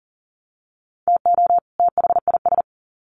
Morse code, heard as a keyed tone, about 40 wpm, 700 Hz. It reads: TO THIS
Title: Sets of 2 Words 40wpm